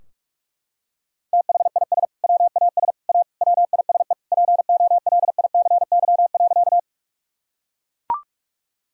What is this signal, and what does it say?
Morse code, heard as a keyed tone, about 43 wpm, 700 Hz. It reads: THIS WAS A WISE POLICY.
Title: Sentences from Top 1600 Words 45wpm